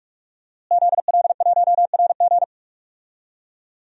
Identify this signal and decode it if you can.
Morse code, heard as a keyed tone, about 46 wpm, 700 Hz. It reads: ZL1RG
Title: DXCC-100 Call Signs - 5 Characters - Repeated 2x - 45wpm